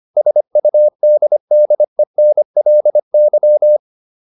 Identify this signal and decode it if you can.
Morse code, heard as a keyed tone, about 25 wpm, 600 Hz. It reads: SUDDENLY
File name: LY8 25